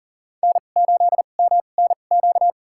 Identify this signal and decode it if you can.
Morse code, heard as a keyed tone, about 40 wpm, spaced as for 29 wpm, 700 Hz. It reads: N8MNQ
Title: USA Call Signs - 5 Characters - 2x Character Spacing - 40wpm